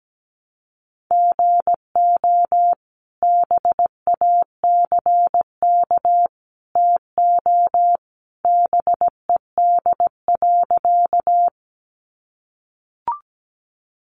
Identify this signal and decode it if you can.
Morse code, heard as a keyed tone, about 17 wpm, 700 Hz. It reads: GO BACK TO BED.
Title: Sentences from Top 500 Words 17wpm